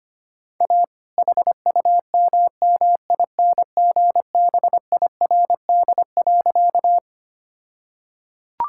Morse code, heard as a keyed tone, about 25 wpm, 700 Hz. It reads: A HUMMINGBIRD.
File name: practice / 25wpm / All-Age Appropriate Jokes - 25wpm